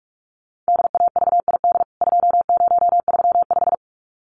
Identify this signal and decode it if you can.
Morse code, heard as a keyed tone, about 44 wpm, 700 Hz. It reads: DAVID 2035
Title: CWT Exchange 45wpm